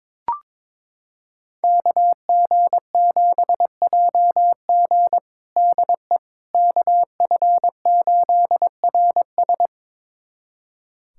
Morse code, heard as a keyed tone, about 22 wpm, 700 Hz. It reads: KG7JG DE KF8RS
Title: USA Callsign DE Callsign - 5 Characters - 22wpm